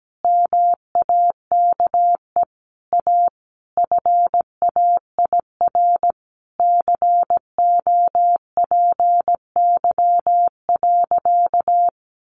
Morse code, heard as a keyed tone, about 17 wpm, 700 Hz. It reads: MAKE A FAIR COPY.
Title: Sentences from Top 1400 Words 17wpm